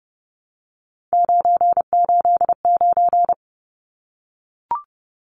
Morse code, heard as a keyed tone, about 30 wpm, 700 Hz. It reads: 989
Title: Numbers - 3-Digit 30wpm